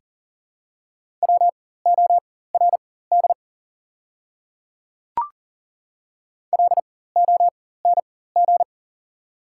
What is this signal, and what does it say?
Morse code, heard as a keyed tone, about 37 wpm, spaced as for 18 wpm, 700 Hz. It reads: WORD LONG
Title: 4x Character Spacing - Top 100 Words - 40wpm